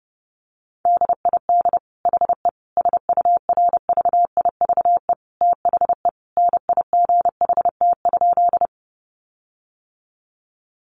Morse code, heard as a keyed tone, about 30 wpm, 700 Hz. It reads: DID HE SURVIVE THE NIGHT?
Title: Sentences from Top 1300 Words 30wpm